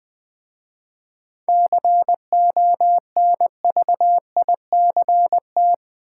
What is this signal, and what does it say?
Morse code, heard as a keyed tone, about 20 wpm, 700 Hz. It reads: CONVICT
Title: 7 Letter Words - First 3 Letters Given - 20wpm